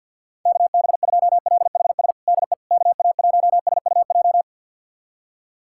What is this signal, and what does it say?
Morse code, heard as a keyed tone, about 52 wpm, 700 Hz. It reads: KB2LHS DE KA1SUJ
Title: USA Callsign DE Callsign 50wpm